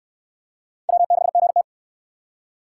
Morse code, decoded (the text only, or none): W6GT